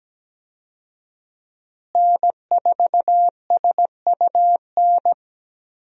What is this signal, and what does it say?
Morse code, heard as a keyed tone, about 17 wpm, 700 Hz. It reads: N4SUN
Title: USA Call Signs - 5 Characters - Repeated 3x - 17wpm